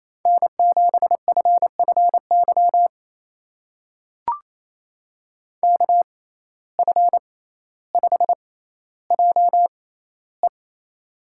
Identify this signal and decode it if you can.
Morse code, heard as a keyed tone, about 28 wpm, spaced as for 14 wpm, 700 Hz. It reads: N7FFY KF5JE
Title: USA Call Signs - 5 Characters - 6x Character Spacing - 28wpm